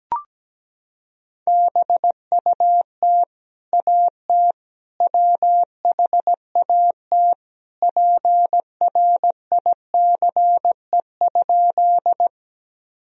Morse code, heard as a keyed tone, about 17 wpm, 700 Hz. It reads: BUT AT WHAT PRICE?
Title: Sentences from Top 1000 Words - Review Part 4 - 17wpm